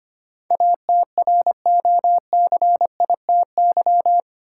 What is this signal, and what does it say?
Morse code, heard as a keyed tone, about 25 wpm, 700 Hz. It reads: ATROCITY
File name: 8 Letter Words - First 3 Letters Given - 25wpm